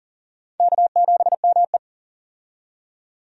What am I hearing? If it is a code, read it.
Morse code, heard as a keyed tone, about 40 wpm, 700 Hz. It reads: K7ME
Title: USA Call Signs - 4 Characters - 40wpm